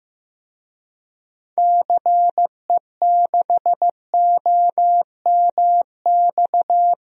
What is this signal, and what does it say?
Morse code, heard as a keyed tone, about 15 wpm, 700 Hz. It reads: CE6OMX
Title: DXCC-100 Call Signs - 6 Characters - Repeated 2x - 15wpm